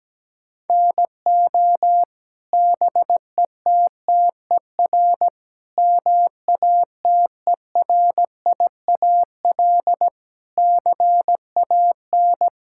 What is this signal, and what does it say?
Morse code, heard as a keyed tone, about 17 wpm, 700 Hz. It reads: NO BETTER MATERIAL CAN
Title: Sentences from Top 1000 Words - Review Part 5 - 17wpm